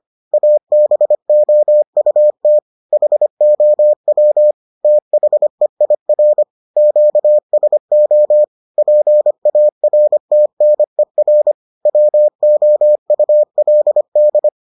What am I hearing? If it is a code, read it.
Morse code, heard as a keyed tone, about 25 wpm, 600 Hz. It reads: ABOUT HOW THEIR QSO PARTNER WOULD